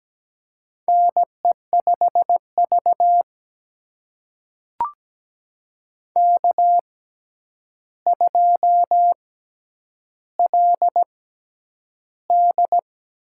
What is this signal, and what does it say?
Morse code, heard as a keyed tone, about 17 wpm, spaced as for 9 wpm, 700 Hz. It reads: NE5V K2LD
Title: USA Call Signs - 4 Characters - 6x Character Spacing - 17wpm